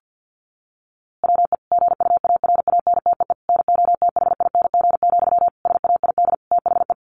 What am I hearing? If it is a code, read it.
Morse code, heard as a keyed tone, about 50 wpm, 700 Hz. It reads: WE GUARANTEE NOTHING, SAID THE
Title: A Sound Of Thunder 50wpm